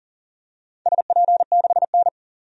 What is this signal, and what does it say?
Morse code, heard as a keyed tone, about 40 wpm, 700 Hz. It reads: SP6N
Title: DXCC-100 Call Signs - 4 Characters - 40wpm